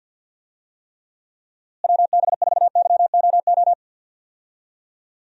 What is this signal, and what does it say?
Morse code, heard as a keyed tone, about 52 wpm, 700 Hz. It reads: WB4YOO
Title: USA Call Signs - 6 Characters - Repeated 3x - 50wpm